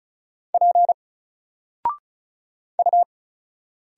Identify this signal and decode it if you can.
Morse code, heard as a keyed tone, about 35 wpm, 700 Hz. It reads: P U
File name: Single-Letter-Number-35wpm